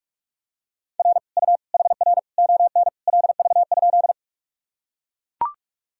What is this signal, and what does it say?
Morse code, heard as a keyed tone, about 43 wpm, 700 Hz. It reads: R U HR ON LV?
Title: Sentences from Top 400 Words and Abbreviations 45wpm